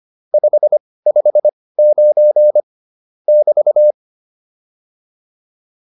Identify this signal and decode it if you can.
Morse code, heard as a keyed tone, about 25 wpm, spaced as for 18 wpm, 600 Hz. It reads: 559 <BT>